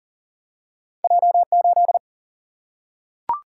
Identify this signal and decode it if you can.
Morse code, heard as a keyed tone, about 40 wpm, 700 Hz. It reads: J 8
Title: Single-Letter Single-Number 40wpm